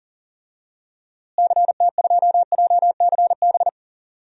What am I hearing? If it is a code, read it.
Morse code, heard as a keyed tone, about 40 wpm, 700 Hz. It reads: CT2JCB
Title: DXCC-100 Call Signs - 6 Characters - Repeated 2x - 40wpm